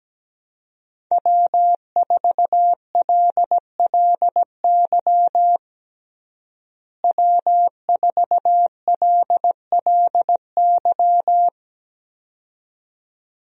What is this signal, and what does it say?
Morse code, heard as a keyed tone, about 17 wpm, 700 Hz. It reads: W4LLY W4LLY
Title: USA Call Signs - 5 Characters - Repeated 2x - 17wpm